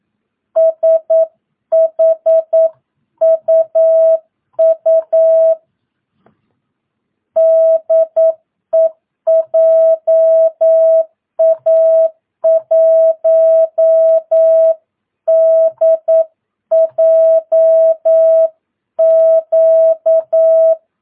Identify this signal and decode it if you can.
Morse code, heard as a keyed tone, about 9 wpm, 660 Hz. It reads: SHUU DE JA1DJQ
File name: jh8cbh / 2022cw